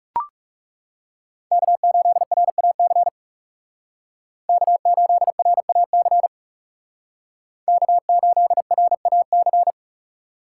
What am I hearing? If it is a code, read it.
Morse code, heard as a keyed tone, about 36 wpm, 700 Hz. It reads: K8RAC K8RAC K8RAC
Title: Speed Racing - USA 1x3 Call Signs 30wpm